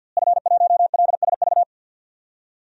Morse code, heard as a keyed tone, about 52 wpm, 700 Hz. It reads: F1LSV
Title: DXCC-100 Call Signs - 5 Characters - Repeated 2x - 50wpm